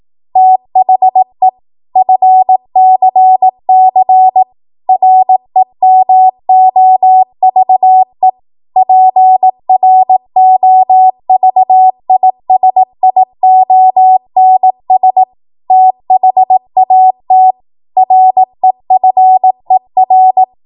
Morse code, heard as a keyed tone, about 18 wpm, 750 Hz. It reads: THE FCC REMOVE PROVISIONS THAT REFER
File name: Morse / 18WPM